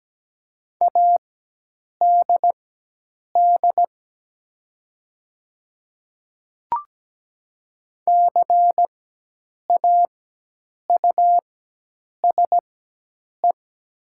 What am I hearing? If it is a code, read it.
Morse code, heard as a keyed tone, about 17 wpm, spaced as for 8 wpm, 690 Hz. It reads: ADD CAUSE
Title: 4x Character Spacing - Top 300 Words - 17wpm